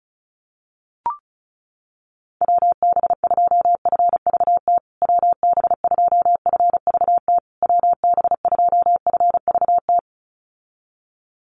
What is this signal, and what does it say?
Morse code, heard as a keyed tone, about 34 wpm, 700 Hz. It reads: WB2FVT WB2FVT WB2FVT
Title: USA Call Signs Repeated 3x 35wpm